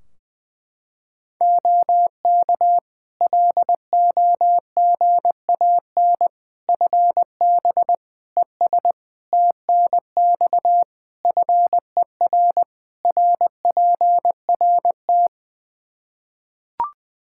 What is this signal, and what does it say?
Morse code, heard as a keyed tone, about 20 wpm, 700 Hz. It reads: OK LOGAN FB ES TNX FER RPRT
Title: Practice QSO - Exchange 2 - Info - 20wpm